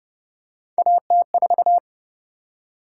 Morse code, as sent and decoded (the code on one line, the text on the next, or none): .- - ....-
AT4